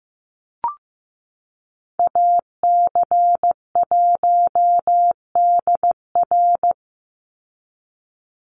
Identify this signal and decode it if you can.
Morse code, heard as a keyed tone, about 15 wpm, 700 Hz. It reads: AC1DR
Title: USA Call Signs 15wpm